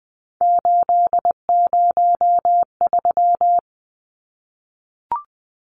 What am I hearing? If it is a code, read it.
Morse code, heard as a keyed tone, about 20 wpm, 700 Hz. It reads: 803
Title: Numbers - 3-Digit 20wpm